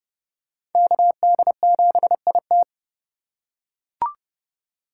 Morse code, heard as a keyed tone, about 30 wpm, 700 Hz. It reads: KD7IT
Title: USA Call Signs - 2x Character Spacing - 30wpm